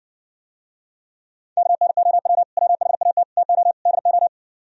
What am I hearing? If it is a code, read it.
Morse code, heard as a keyed tone, about 59 wpm, 700 Hz. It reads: KNOW WHAT TO DO